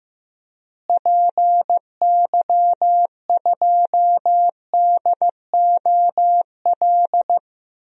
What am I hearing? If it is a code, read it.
Morse code, heard as a keyed tone, about 15 wpm, 700 Hz. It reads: PY2DOL
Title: DXCC-100 Call Signs - 6 Characters - Repeated 2x - 15wpm